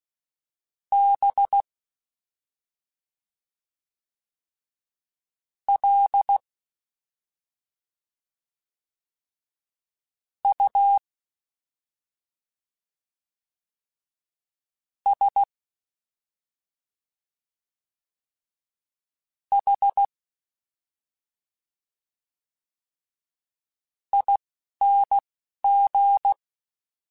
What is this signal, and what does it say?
Morse code, heard as a keyed tone, about 16 wpm, spaced as for 3 wpm, 800 Hz. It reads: BLUSHING
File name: Special INGs-mp3s / Special INGs-mp3s / Slow ing @10